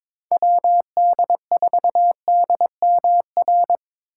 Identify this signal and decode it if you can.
Morse code, heard as a keyed tone, about 26 wpm, 700 Hz. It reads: WD4DMR
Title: Speed Racing - USA 2x3 Call Signs 22wpm